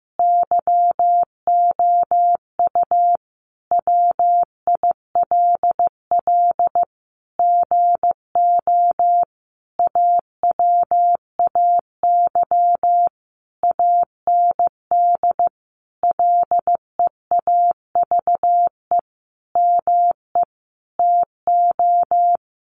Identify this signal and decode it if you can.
Morse code, heard as a keyed tone, about 15 wpm, 700 Hz. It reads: YOU WILL GO AWAY AND LEAVE ME TO
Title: Sentences from Top 1300 Words 15wpm